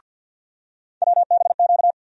Morse code, heard as a keyed tone, about 51 wpm, 700 Hz. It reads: WBQ
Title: USA Call Sign Postfixes - 3 Letters 50wpm